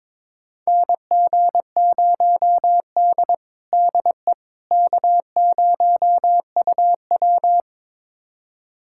Morse code, heard as a keyed tone, about 22 wpm, 700 Hz. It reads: NG0D DE K0UW
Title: USA Callsign DE Callsign - 4 Characters - 22wpm